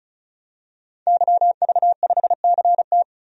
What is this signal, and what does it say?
Morse code, heard as a keyed tone, about 34 wpm, 700 Hz. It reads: YV5CT
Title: DXCC-100 Call Signs - 5 Characters - 35wpm